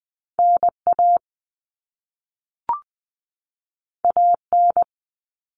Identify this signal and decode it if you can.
Morse code, heard as a keyed tone, about 20 wpm, 700 Hz. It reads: NA AN
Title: All Cut Numbers - 2-Digit 20wpm